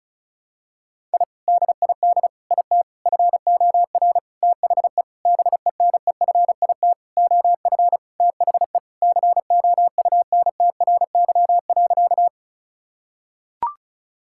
Morse code, heard as a keyed tone, about 36 wpm, 690 Hz. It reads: I DID IT FOR THE BENEFIT OF THE COUNTRY.
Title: Sentences from Top 1000 Words - Review Part 6 - 35wpm